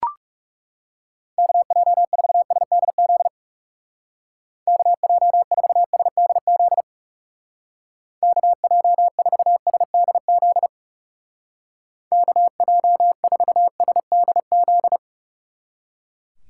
Morse code, heard as a keyed tone, about 38 wpm, 700 Hz. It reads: KJ4SDZ KJ4SDZ KJ4SDZ KJ4SDZ
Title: Speed Racing - USA 2x3 Call Signs 30wpm